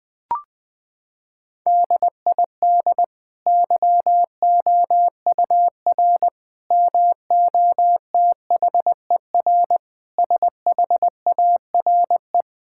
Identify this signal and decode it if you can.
Morse code, heard as a keyed tone, about 19 wpm, 700 Hz. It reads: DID YOUR MOTHER SHARE
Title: Sentences from Top 1000 Words - Review Part 6 - 20wpm